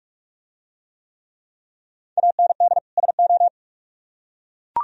Morse code, heard as a keyed tone, about 43 wpm, 700 Hz. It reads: AND SO